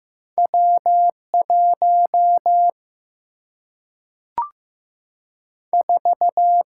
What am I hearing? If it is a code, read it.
Morse code, heard as a keyed tone, about 15 wpm, 700 Hz. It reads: W 1 4
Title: Single-Letter Single-Number 15wpm